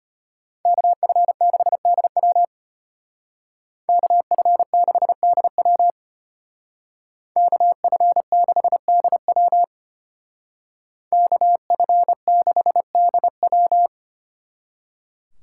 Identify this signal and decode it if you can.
Morse code, heard as a keyed tone, about 30 wpm, 700 Hz. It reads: KF6DW KF6DW KF6DW KF6DW
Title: Speed Racing - USA 2x2 Call Signs 25wpm